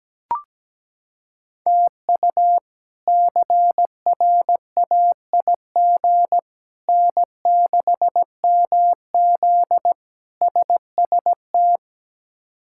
Morse code, heard as a keyed tone, about 17 wpm, 700 Hz. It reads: TU CRAIG N6MZ SST
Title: K1USN SST QSO Practice - 17wpm